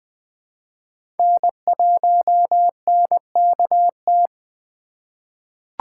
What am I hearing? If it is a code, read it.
Morse code, heard as a keyed tone, about 20 wpm, 700 Hz. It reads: N1NKT
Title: USA Call Signs - 5 Characters - 4x Character Spacing - 20wpm